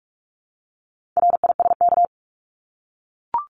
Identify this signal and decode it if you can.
Morse code, heard as a keyed tone, about 44 wpm, 700 Hz. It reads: RISK
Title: Top 501-750 Words 45wpm